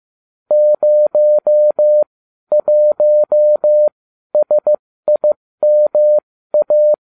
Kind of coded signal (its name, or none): Morse code